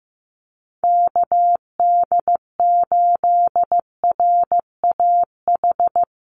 Morse code, heard as a keyed tone, about 15 wpm, 700 Hz. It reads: KD8RAH
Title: USA Call Signs 15wpm